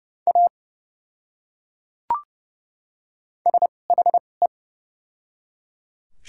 Morse code, heard as a keyed tone, about 30 wpm, spaced as for 5 wpm, 700 Hz. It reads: AS HE
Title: 2x Character Spacing - Top 100 Words - 30wpm